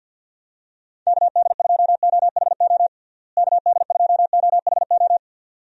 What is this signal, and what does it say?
Morse code, heard as a keyed tone, about 48 wpm, 700 Hz. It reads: KD2OHO KD2OHO
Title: USA Call Signs - 6 Characters - Repeated 2x - 50wpm